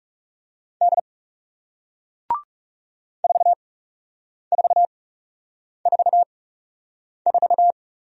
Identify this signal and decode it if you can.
Morse code, heard as a keyed tone, about 37 wpm, spaced as for 8 wpm, 700 Hz. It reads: D 4444
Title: Speed-Racing Single Letter-Number 30wpm